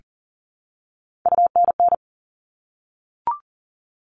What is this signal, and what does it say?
Morse code, heard as a keyed tone, about 40 wpm, 700 Hz. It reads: UNN
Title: All Cut Numbers - 3-Digit 40wpm